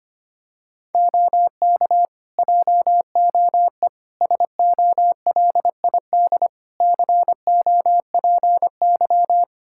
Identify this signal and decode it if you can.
Morse code, heard as a keyed tone, about 25 wpm, 700 Hz. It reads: OK JOE SOLID COPY
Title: Practice QSO - Exchange 3 - More Info - 25wpm